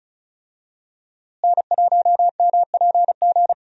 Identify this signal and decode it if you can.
Morse code, heard as a keyed tone, about 34 wpm, 700 Hz. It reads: N1MPG
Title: USA Call Signs - 5 Characters - Repeated 2x - 35wpm